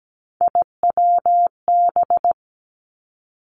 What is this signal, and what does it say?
Morse code, heard as a keyed tone, about 17 wpm, 700 Hz. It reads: IWB
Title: Three Challenging Letter-Number Combinations 17wpm